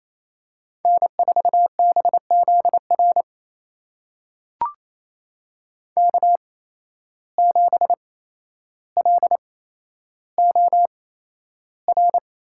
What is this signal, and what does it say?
Morse code, heard as a keyed tone, about 28 wpm, spaced as for 12 wpm, 700 Hz. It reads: N4BZR K7LOR
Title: USA Call Signs - 5 Characters - 8x Character Spacing - 28wpm